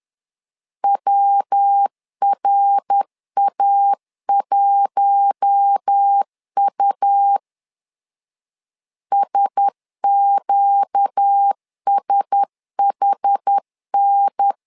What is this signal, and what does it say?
Morse code, heard as a keyed tone, about 10 wpm, 800 Hz. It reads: WRA1U SQSHN
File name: LEZIONI / 4 Q1S 10wpm 2133